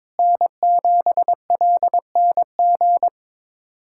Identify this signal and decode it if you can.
Morse code, heard as a keyed tone, about 22 wpm, 700 Hz. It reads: N7LNG
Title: USA Callsign DE Callsign - 5 Characters - 22wpm